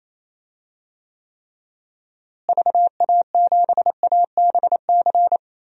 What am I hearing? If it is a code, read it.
Morse code, heard as a keyed tone, about 28 wpm, 700 Hz. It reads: VA7ABC
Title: DXCC-100 Call Signs - 6 Characters - Repeated 2x - 28wpm